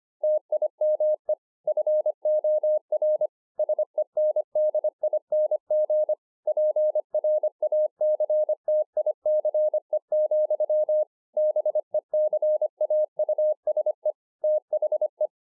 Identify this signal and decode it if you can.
Morse code, heard as a keyed tone, about 25 wpm, 610 Hz. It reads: TIME FOR SENDING PRACTICE, BECAUSE THE
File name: 2011-04-mp3 / THIS IS NOT MY SHACK 2 25WPM